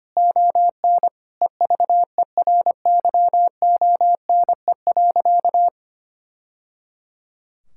Morse code, heard as a keyed tone, about 25 wpm, 700 Hz. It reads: ON EVERYONE.
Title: Sentences from Top 1000 Words - Review Part 6 - 25wpm